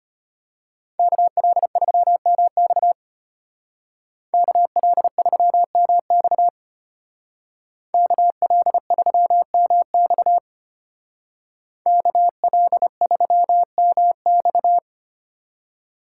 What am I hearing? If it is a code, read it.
Morse code, heard as a keyed tone, about 32 wpm, 700 Hz. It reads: KL3MX KL3MX KL3MX KL3MX